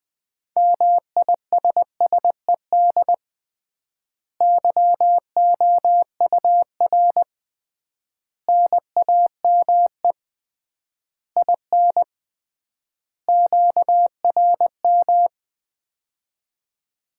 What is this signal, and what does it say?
Morse code, heard as a keyed tone, about 21 wpm, 700 Hz. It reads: MISSED YOUR NAME IN QRM
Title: Common QSO Phrases - 3x Word-Spacing - 20wpm